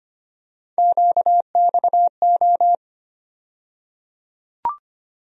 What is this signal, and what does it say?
Morse code, heard as a keyed tone, about 25 wpm, 700 Hz. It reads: QXO
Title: USA Call Sign Postfixes - 3 Letters 25wpm